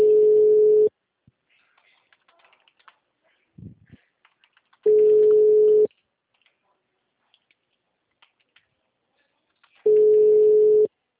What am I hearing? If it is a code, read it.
Ringback tone (Europe, 425 Hz).
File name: out- -2017- - - .30534